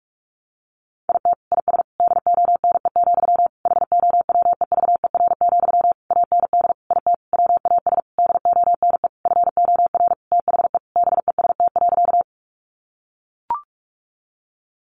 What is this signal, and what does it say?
Morse code, heard as a keyed tone, about 44 wpm, 700 Hz. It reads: IT IS DONE, HOWEVER, AND IT WAS DONE FOR THE BEST.
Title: Sentences from Top 600 Words 45wpm